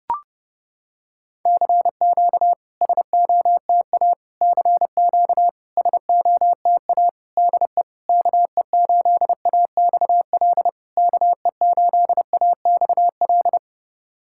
Morse code, heard as a keyed tone, about 32 wpm, 690 Hz. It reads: CQ SOTA CQ SOTA DE KE8AXL KE8AXL
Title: SOTA QSO Practice 30wpm